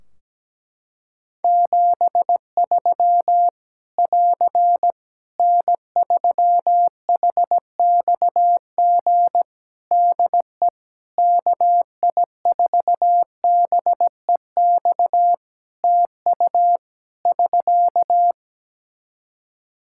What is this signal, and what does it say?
Morse code, heard as a keyed tone, about 17 wpm, 700 Hz. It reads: 73 <AR> N3HXG DE KI4BEX TU <SK>
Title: K1USN SST QSO Practice 17wpm